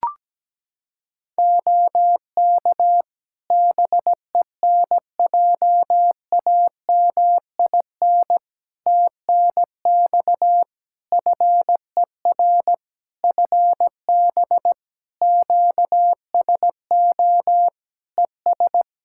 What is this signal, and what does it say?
Morse code, heard as a keyed tone, about 17 wpm, 700 Hz. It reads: OK BENJAMIN TNX FER FB QSO ES
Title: Practice QSO - Exchange 4 - Ending - 17wpm